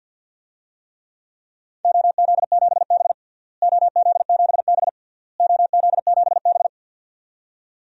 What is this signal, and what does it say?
Morse code, heard as a keyed tone, about 48 wpm, 700 Hz. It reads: OZ7B OZ7B OZ7B
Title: DXCC-100 Call Signs - 4 Characters - Repeated 3x - 50wpm